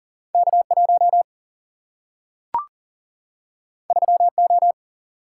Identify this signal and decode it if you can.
Morse code, heard as a keyed tone, about 40 wpm, 700 Hz. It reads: K1 3O
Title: Single-Letter Single-Number 40wpm